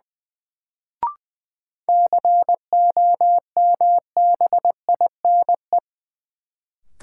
Morse code, heard as a keyed tone, about 21 wpm, 700 Hz. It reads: COMBINE
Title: Sentences from Top 1600 Words - New Word Drill Alt - 20wpm